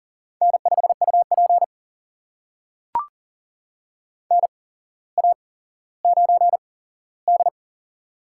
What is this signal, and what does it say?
Morse code, heard as a keyed tone, about 40 wpm, spaced as for 17 wpm, 700 Hz. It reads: N5UP NA9D
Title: USA Call Signs - 4 Characters - 8x Character Spacing - 40wpm